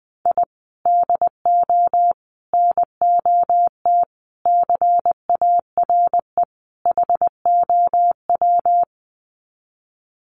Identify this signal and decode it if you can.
Morse code, heard as a keyed tone, about 20 wpm, 700 Hz. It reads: I DO NOT CARE HOW
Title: Sets of 5 Words Encore2 20wpm